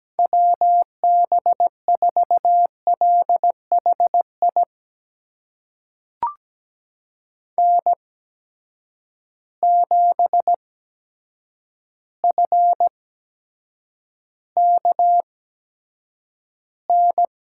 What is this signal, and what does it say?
Morse code, heard as a keyed tone, about 17 wpm, spaced as for 8 wpm, 700 Hz. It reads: WB4LHI N7FKN
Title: USA Call Signs - 8x Character Spacing - 17wpm